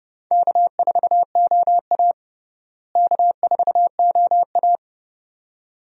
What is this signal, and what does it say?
Morse code, heard as a keyed tone, about 30 wpm, 700 Hz. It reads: K4OA K4OA
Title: USA Call Signs - 4 Characters - Repeated 2x - 30wpm